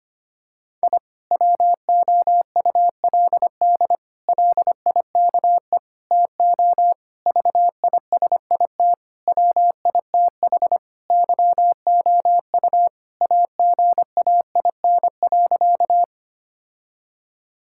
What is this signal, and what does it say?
Morse code, heard as a keyed tone, about 25 wpm, 700 Hz. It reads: I WOULD LIKE TO VISIT WITH YOU AGAIN.
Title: Sentences from Top 1000 Words - Review Part 6 - 25wpm